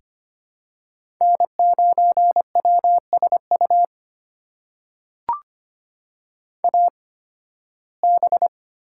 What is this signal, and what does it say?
Morse code, heard as a keyed tone, about 25 wpm, spaced as for 16 wpm, 700 Hz. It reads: N9WSU AB
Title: USA Call Signs - 5 Characters - 8x Character Spacing - 25wpm